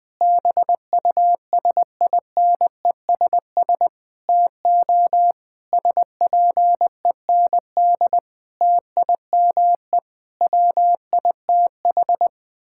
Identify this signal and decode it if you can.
Morse code, heard as a keyed tone, about 21 wpm, 700 Hz. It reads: BUSINESS TO SPEND TIME WITH
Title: Sentences from Top 1000 Words - Review Part 6 - 20wpm